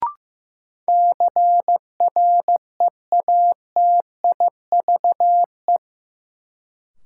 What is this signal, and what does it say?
Morse code, heard as a keyed tone, about 16 wpm, 700 Hz. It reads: CREATIVE
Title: Sentences from Top 1000 Words - Review Part 5 - New Word Drill - 15wpm